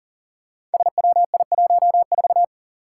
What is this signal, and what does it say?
Morse code, heard as a keyed tone, about 39 wpm, 700 Hz. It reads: SWI14
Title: Five Challenging Letter-Number Combinations 40wpm